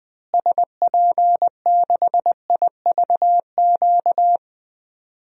Morse code, heard as a keyed tone, about 21 wpm, 700 Hz. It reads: SP6IVQ
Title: DXCC-100 Call Signs - 6 Characters - Repeated 2x - 20wpm